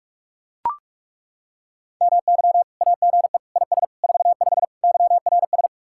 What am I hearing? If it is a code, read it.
Morse code, heard as a keyed tone, about 38 wpm, 700 Hz. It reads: MY AGE IS 45 YRS